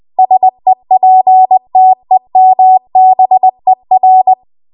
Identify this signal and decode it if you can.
Morse code, heard as a keyed tone, about 20 wpm, 750 Hz. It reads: SEPTEMBER
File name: Archive / 20 WPM / 20WPM